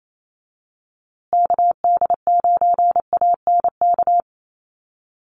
Morse code, heard as a keyed tone, about 28 wpm, 700 Hz. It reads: KD9ANK
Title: USA Call Signs Repeated 3x 28wpm